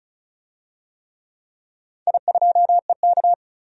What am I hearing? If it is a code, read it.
Morse code, heard as a keyed tone, about 36 wpm, 700 Hz. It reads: I2EK
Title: DXCC-100 Call Signs - 4 Characters - 35wpm